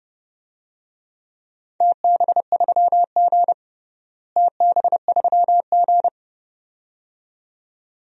Morse code, heard as a keyed tone, about 30 wpm, 700 Hz. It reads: TB3G TB3G
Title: DXCC-100 Call Signs - 4 Characters - Repeated 2x - 30wpm